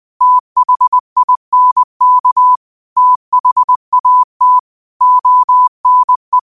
Morse code, heard as a keyed tone, about 20 wpm, 1000 Hz. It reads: THINK THAT ONE
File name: qotdfiles / 20wpm